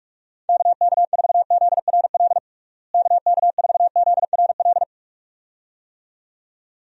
Morse code, heard as a keyed tone, about 46 wpm, 700 Hz. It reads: KK4ZRL KK4ZRL
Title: USA Call Signs - 6 Characters - Repeated 2x - 45wpm